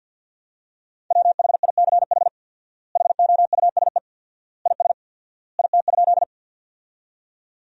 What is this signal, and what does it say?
Morse code, heard as a keyed tone, about 50 wpm, 700 Hz. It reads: WHICH HOUSE IS IT?
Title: 4x Word-Spacing - Sentences from Top 200 Words - 50wpm